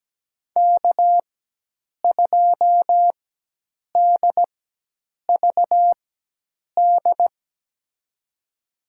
Morse code, heard as a keyed tone, about 17 wpm, spaced as for 8 wpm, 700 Hz. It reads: K2DVD
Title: USA Call Signs - 5 Characters - 4x Character Spacing - 17wpm